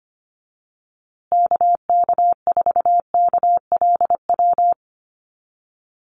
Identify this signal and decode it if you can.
Morse code, heard as a keyed tone, about 25 wpm, 700 Hz. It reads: KK4KLW
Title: USA Call Signs Repeated 3x 25wpm